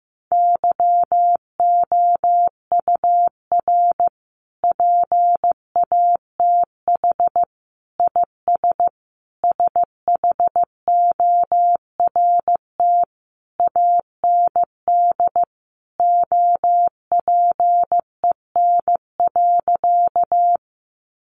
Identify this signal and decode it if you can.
Morse code, heard as a keyed tone, about 15 wpm, 700 Hz. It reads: YOUR PATH IS SHORT AND OPEN.
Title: Sentences from Top 1400 Words 15wpm